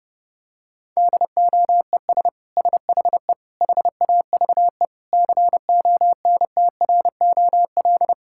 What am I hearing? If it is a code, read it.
Morse code, heard as a keyed tone, about 30 wpm, 700 Hz. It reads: DOES SHE HAVE CONTROL
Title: Sentences from Top 1000 Words - Review Part 5 - 30wpm